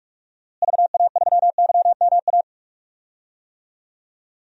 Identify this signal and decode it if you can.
Morse code, heard as a keyed tone, about 46 wpm, 700 Hz. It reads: VA3YMA
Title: DXCC-100 Call Signs - 6 Characters - Repeated 2x - 45wpm